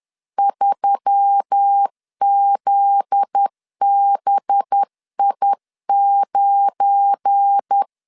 Morse code, heard as a keyed tone, about 10 wpm, 800 Hz. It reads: 3ZBI9
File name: LEZIONI / 8 YJ45 / 8 Y 10wpm 2615